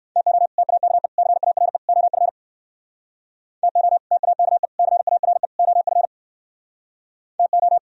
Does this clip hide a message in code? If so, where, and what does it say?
Yes: Morse code from 0.2 to 7.8 s, heard as a keyed tone, about 64 wpm, 700 Hz. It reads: TO TAKE CARE OF TO TAKE CARE OF TO